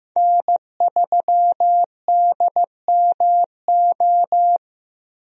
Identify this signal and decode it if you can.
Morse code, heard as a keyed tone, about 15 wpm, 700 Hz. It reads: N3DMO
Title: USA Call Signs - 5 Characters - Repeated 3x - 15wpm